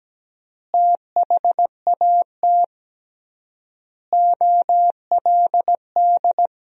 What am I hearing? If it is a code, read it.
Morse code, heard as a keyed tone, about 17 wpm, 700 Hz. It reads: THAT OLD
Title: 3x Word-Spacing - Sentences from Top 400 Words - 17wpm